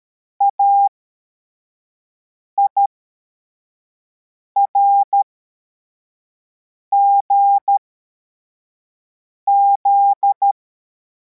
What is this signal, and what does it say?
Morse code, heard as a keyed tone, about 13 wpm, spaced as for 4 wpm, 800 Hz. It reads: AIRGZ
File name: formation / cw / Text44